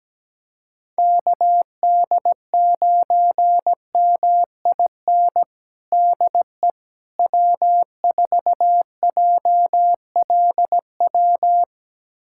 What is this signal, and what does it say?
Morse code, heard as a keyed tone, about 17 wpm, 700 Hz. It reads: KD9MIN DE W4JLW
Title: USA Callsign DE Callsign 17wpm